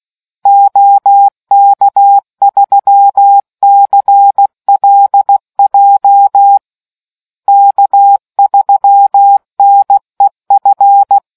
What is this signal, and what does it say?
Morse code, heard as a keyed tone, about 16 wpm, 800 Hz. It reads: OK3CLJ K3NEF